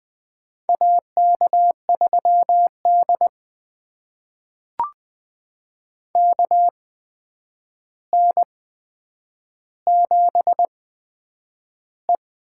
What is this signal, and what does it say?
Morse code, heard as a keyed tone, about 20 wpm, spaced as for 9 wpm, 700 Hz. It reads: AK3D KN7E
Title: USA Call Signs - 4 Characters - 8x Character Spacing - 20wpm